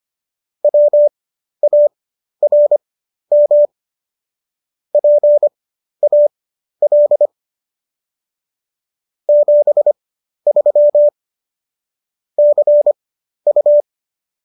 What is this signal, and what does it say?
Morse code, heard as a keyed tone, about 25 wpm, spaced as for 12 wpm, 590 Hz. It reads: WARM PAL 73 CU